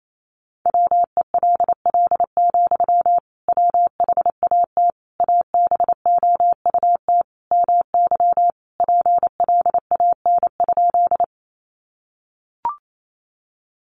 Morse code, heard as a keyed tone, about 28 wpm, 700 Hz. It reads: WELL, WHAT ABOUT MY PLAN?
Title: Sentences from Top 500 Words 28wpm